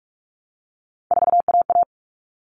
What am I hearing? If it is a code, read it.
Morse code, heard as a keyed tone, about 44 wpm, 700 Hz. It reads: 4AA
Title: All Cut Numbers - 3-Digit 45wpm